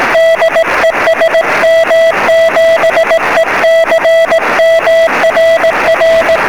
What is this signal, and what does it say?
Morse code, heard as a keyed tone, about 18 wpm, 660 Hz, with noise. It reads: DE SM7ECM RR
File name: sm7lcb / 12-sm7ecm 23cm